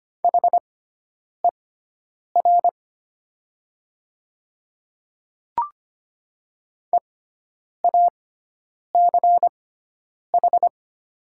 Morse code, heard as a keyed tone, about 25 wpm, spaced as for 9 wpm, 700 Hz. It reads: HER EACH
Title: 6x Character Spacing - Top 100 Words - 25wpm